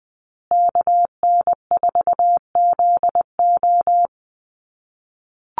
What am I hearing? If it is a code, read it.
Morse code, heard as a keyed tone, about 20 wpm, 700 Hz. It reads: KN4ZO
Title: USA Call Signs Repeated 2x 20wpm